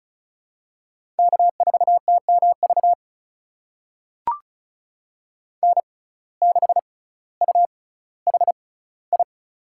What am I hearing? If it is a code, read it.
Morse code, heard as a keyed tone, about 36 wpm, spaced as for 18 wpm, 700 Hz. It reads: K4TMV N6UHI
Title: USA Call Signs - 5 Characters - 6x Character Spacing - 35wpm